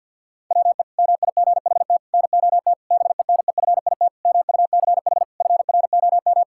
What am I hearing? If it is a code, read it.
Morse code, heard as a keyed tone, about 53 wpm, 700 Hz. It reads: WE MIGHT NOT BENEFIT MUCH FROM
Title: Sentences from Top 1000 Words - Review Part 6 - 50wpm